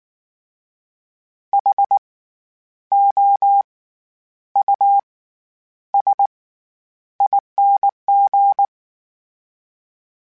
Morse code, heard as a keyed tone, about 19 wpm, spaced as for 9 wpm, 800 Hz. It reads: HOUSING
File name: Special INGs / Special INGs-mp3s / SlowToFaster ing @18